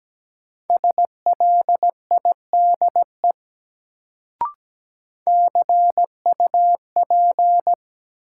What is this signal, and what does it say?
Morse code, heard as a keyed tone, about 17 wpm, 700 Hz. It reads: SLIDE CUP
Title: Sentences from Top 1600 Words - New Word Drill Alt - 17wpm